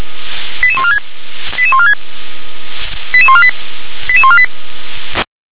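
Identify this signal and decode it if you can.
Five-tone selective call (ZVEI), heard as quick runs of five tones, tones of about 70 ms. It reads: 88146 88146 88148 88148
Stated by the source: Five-tone ZVEI selective call, code 88645, received by SDR.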